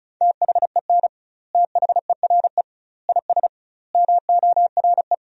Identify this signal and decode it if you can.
Morse code, heard as a keyed tone, about 36 wpm, 700 Hz. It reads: THEN THERE IS MORE
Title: 2x Word-Spacing - Sentences from Top 100 Words - 35wpm